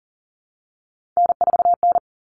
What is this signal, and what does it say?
Morse code, heard as a keyed tone, about 40 wpm, 700 Hz. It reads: N4N
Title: All Cut Numbers - 3-Digit 40wpm